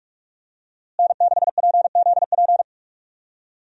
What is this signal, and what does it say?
Morse code, heard as a keyed tone, about 43 wpm, 700 Hz. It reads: N6PZP
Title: USA Call Signs - 5 Characters - Repeated 2x - 45wpm